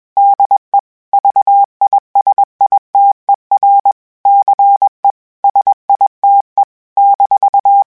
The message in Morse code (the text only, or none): DE VISITER CE SITE -